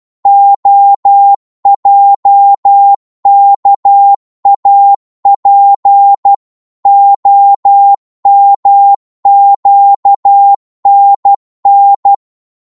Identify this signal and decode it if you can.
Morse code, heard as a keyed tone, about 12 wpm, 800 Hz. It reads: OJKAP OMQNN
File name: Morse / ltr-12wpm